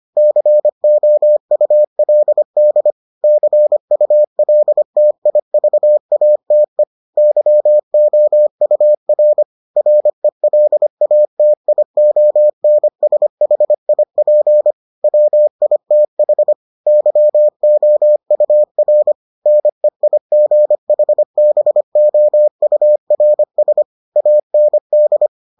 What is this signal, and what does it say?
Morse code, heard as a keyed tone, about 25 wpm, 600 Hz. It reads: COULD CULTIVATE YOUR RELATIONSHIP WITH YOUR NEIGHBOURS AND